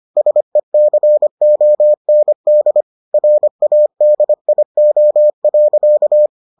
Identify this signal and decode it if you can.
Morse code, heard as a keyed tone, about 25 wpm, 600 Hz. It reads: SECOND RADIO.